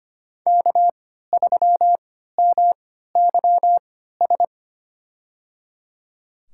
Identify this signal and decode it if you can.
Morse code, heard as a keyed tone, about 25 wpm, spaced as for 14 wpm, 700 Hz. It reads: K3MYS
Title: USA Call Signs - 5 Characters - 3x Character Spacing - 25wpm